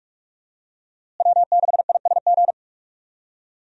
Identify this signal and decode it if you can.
Morse code, heard as a keyed tone, about 41 wpm, 700 Hz. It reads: W6ISG
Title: USA Call Signs - 5 Characters - Repeated 2x - 45wpm